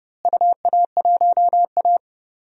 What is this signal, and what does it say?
Morse code, heard as a keyed tone, about 31 wpm, 700 Hz. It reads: UA1A
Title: DXCC-100 Call Signs - 4 Characters - Repeated 3x - 30wpm